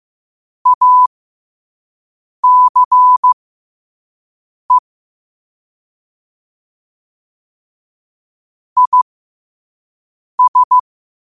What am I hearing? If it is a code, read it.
Morse code, heard as a keyed tone, about 15 wpm, spaced as for 5 wpm, 1000 Hz. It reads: ACE IS
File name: qotdfiles / 5wpm